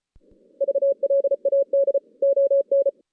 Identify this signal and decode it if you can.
Morse code, heard as a keyed tone, about 32 wpm, 550 Hz. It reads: VLAD ON
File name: 2016 JAN CW NAQP / 2016 Jan naqp k5ot - 0-6